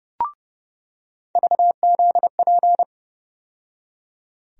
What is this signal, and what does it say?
Morse code, heard as a keyed tone, about 30 wpm, 700 Hz. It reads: VZP
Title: USA Call Sign Postfixes - 3 Letters 30wpm